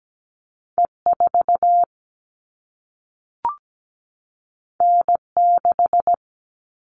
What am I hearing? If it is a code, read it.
Morse code, heard as a keyed tone, about 17 wpm, 700 Hz. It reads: E4 N6
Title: All Cut Numbers - 2-Digit 17wpm